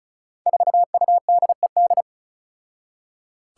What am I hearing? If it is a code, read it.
Morse code, heard as a keyed tone, about 36 wpm, 700 Hz. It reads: 4UDED
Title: Five Challenging Letter-Number Combinations 35wpm